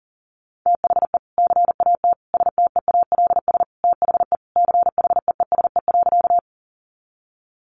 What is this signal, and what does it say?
Morse code, heard as a keyed tone, about 40 wpm, 700 Hz. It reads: THE CAT STEALS THE CHEESE.
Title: Easy Phrases 40wpm